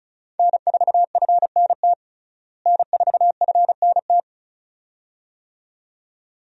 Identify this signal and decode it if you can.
Morse code, heard as a keyed tone, about 34 wpm, 700 Hz. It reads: N4FNT N4FNT
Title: USA Call Signs - 5 Characters - Repeated 2x - 35wpm